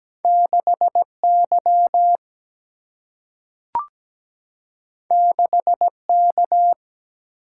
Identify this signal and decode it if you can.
Morse code, heard as a keyed tone, about 17 wpm, 700 Hz. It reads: 6Y 6K
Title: Single-Letter Single-Number 17wpm